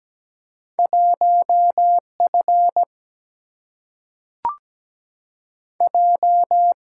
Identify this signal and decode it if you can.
Morse code, heard as a keyed tone, about 17 wpm, 700 Hz. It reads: 1 F J
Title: Single-Letter Single-Number 17wpm